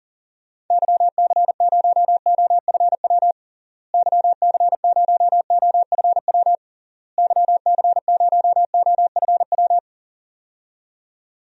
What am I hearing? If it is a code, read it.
Morse code, heard as a keyed tone, about 40 wpm, 700 Hz. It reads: YC0OFW YC0OFW YC0OFW
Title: DXCC-100 Call Signs - 6 Characters - Repeated 3x - 40wpm